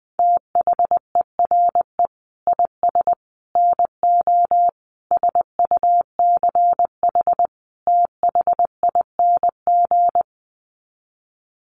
Morse code, heard as a keyed tone, about 20 wpm, 700 Hz. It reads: THERE IS NO SUCH THING
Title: Sets of 5 Words Encore 20wpm